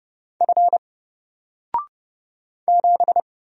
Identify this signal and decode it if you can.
Morse code, heard as a keyed tone, about 30 wpm, 700 Hz. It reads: F 7
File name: Single-Letter-Number-30wpm